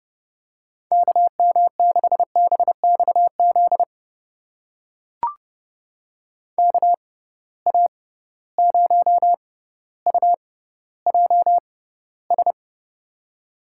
Morse code, heard as a keyed tone, about 30 wpm, spaced as for 15 wpm, 700 Hz. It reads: KM6BXZ KA0UJS
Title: USA Call Signs - 6 Characters - 6x Character Spacing - 30wpm